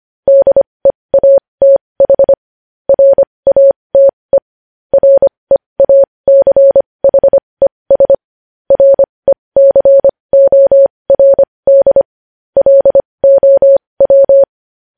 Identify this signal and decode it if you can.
Morse code, heard as a keyed tone, about 25 wpm, spaced as for 20 wpm, 550 Hz. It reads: DEATH RATE REACHES RECORD LOW
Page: news-headlines-in-morse-code-at-20-wpm